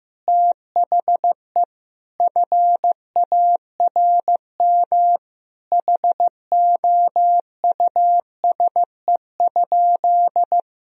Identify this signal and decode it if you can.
Morse code, heard as a keyed tone, about 15 wpm, 700 Hz. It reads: THE FARM HOUSE?
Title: Sentences from Top 1000 Words - Review Part 2 - 15wpm